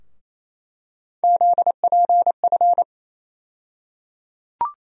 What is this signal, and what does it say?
Morse code, heard as a keyed tone, about 28 wpm, 700 Hz. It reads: ZPF
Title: USA Call Sign Postfixes - 3 Letters 28wpm